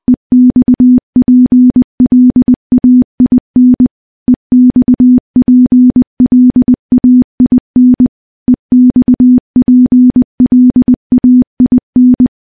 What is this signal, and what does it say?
Morse code, heard as a keyed tone, about 20 wpm, 250 Hz. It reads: EXPLAIN EXPLAIN EXPLAIN
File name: morse-training-20WPM